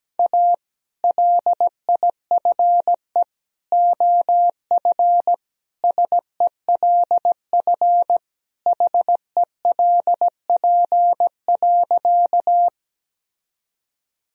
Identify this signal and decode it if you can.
Morse code, heard as a keyed tone, about 17 wpm, 700 Hz. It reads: A LIFE OF SELF HELP.
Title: Sentences from Top 1000 Words - Review Part 1 - 17wpm